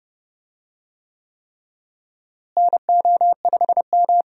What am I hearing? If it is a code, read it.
Morse code, heard as a keyed tone, about 29 wpm, 700 Hz. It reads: NO5M
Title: USA Call Signs - 4 Characters - Repeated 3x - 30wpm